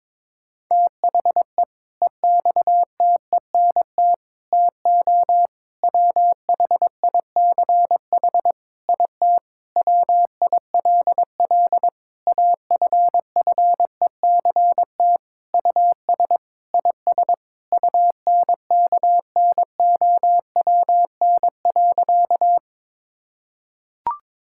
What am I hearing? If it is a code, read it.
Morse code, heard as a keyed tone, about 22 wpm, 700 Hz. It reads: THE EXTENT TO WHICH IT WILL AFFECT US IS UNKNOWN.
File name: practice / 22wpm / Sentences with 4-Word Phrases - 22wpm